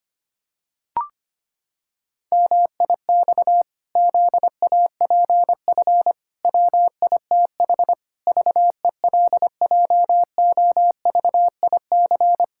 Morse code, heard as a keyed tone, about 25 wpm, 700 Hz. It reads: MIX ZAPF WITH VELJOVIC
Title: Pangram Challenge v2 - 25wpm